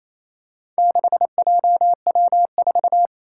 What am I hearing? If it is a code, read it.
Morse code, heard as a keyed tone, about 28 wpm, 700 Hz. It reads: 6JW4
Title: Four Challenging Letter-Number Combinations 28wpm